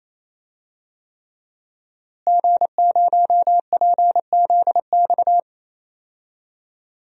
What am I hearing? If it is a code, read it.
Morse code, heard as a keyed tone, about 28 wpm, 700 Hz. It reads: G0PZX
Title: DXCC-100 Call Signs - 5 Characters - Repeated 3x - 28wpm